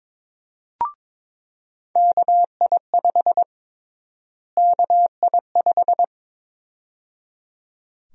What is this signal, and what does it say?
Morse code, heard as a keyed tone, about 21 wpm, 700 Hz. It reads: KI5 KI5
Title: USA Call Sign Prefixes - Repeated 2x - 22wpm